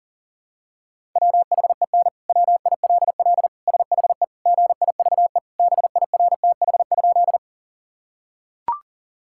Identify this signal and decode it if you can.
Morse code, heard as a keyed tone, about 40 wpm, 700 Hz. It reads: WHEN WILL SHE GIVE BIRTH?
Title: Sentences from Top 1000 Words - Review Part 5 - 40wpm